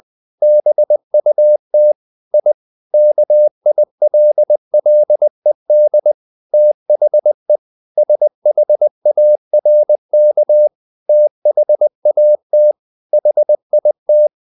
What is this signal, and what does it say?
Morse code, heard as a keyed tone, about 20 wpm, 600 Hz. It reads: BUT I KILLED THE SHARK THAT HIT
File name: the old man and the sea part 2 20wpm